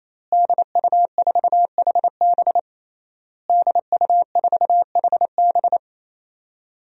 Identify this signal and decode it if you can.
Morse code, heard as a keyed tone, about 27 wpm, 700 Hz. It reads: DU4HB DU4HB
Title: DXCC-100 Call Signs - 5 Characters - Repeated 3x - 28wpm